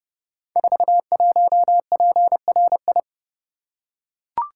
Morse code, heard as a keyed tone, about 30 wpm, 700 Hz. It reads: 41PRI
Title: Code Groups - 5 Letter:Numbers - 30wpm